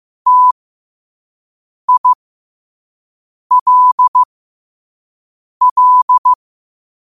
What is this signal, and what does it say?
Morse code, heard as a keyed tone, about 15 wpm, spaced as for 5 wpm, 1000 Hz. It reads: TILL
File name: qotdfiles / 5wpm